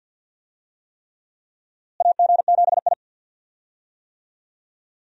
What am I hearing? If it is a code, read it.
Morse code, heard as a keyed tone, about 49 wpm, 700 Hz. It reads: AG7I
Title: USA Call Signs - 4 Characters - Repeated 3x - 50wpm